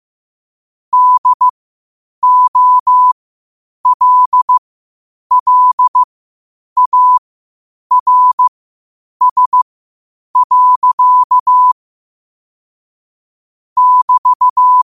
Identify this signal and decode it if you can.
Morse code, heard as a keyed tone, about 15 wpm, spaced as for 8 wpm, 1000 Hz. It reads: DOLLARS. <BT>
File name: qotdfiles / 8wpm